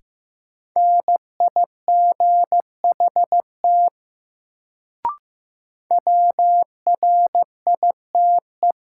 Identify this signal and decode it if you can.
Morse code, heard as a keyed tone, about 15 wpm, 700 Hz. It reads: NIGHT WRITE
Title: Sentences from Top 300 Words - New Word Drill - 15wpm